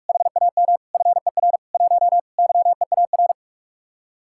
Morse code, heard as a keyed tone, about 44 wpm, 700 Hz. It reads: HAM FER 1 YEAR